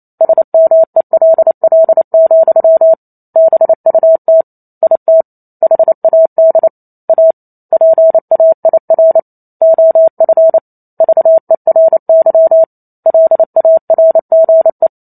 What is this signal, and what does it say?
Morse code, heard as a keyed tone, about 28 wpm, 650 Hz. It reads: SMELL, BUT IT HAD A PAIR OF VERY LARGE